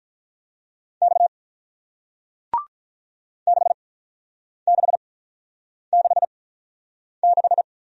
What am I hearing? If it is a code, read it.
Morse code, heard as a keyed tone, about 48 wpm, spaced as for 9 wpm, 700 Hz. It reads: X 6666